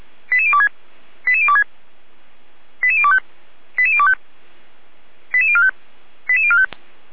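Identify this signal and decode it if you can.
Five-tone selective call (ZVEI), heard as quick runs of five tones, tones of about 70 ms. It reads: 88026 88026 88025 88025 88045 88045
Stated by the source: ZVEI five-tone selective call received by SDR, from feuerwehr-orlamuende.org fire brigade.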